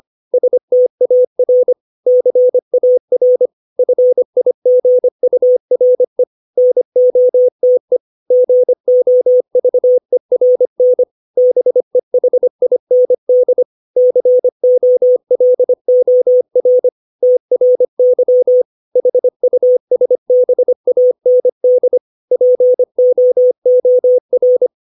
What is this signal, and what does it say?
Morse code, heard as a keyed tone, about 25 wpm, 500 Hz. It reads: STAR CAR FIGURE NOTE GOVERN BEHIND COLOR TRY HUSBAND POOR
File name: 632 words / 632 Words 25WPM